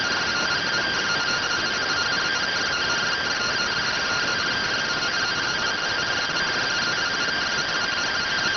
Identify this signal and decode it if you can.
Two-tone teleprinter signal, about 170 Hz shift.